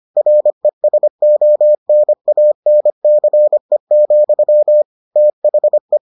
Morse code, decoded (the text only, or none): RESONANCE, THE